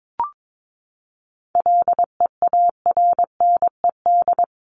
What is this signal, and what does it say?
Morse code, heard as a keyed tone, about 22 wpm, 700 Hz. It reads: LEARNED
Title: ED Suffix - Top 100 Words with Base Words 22wpm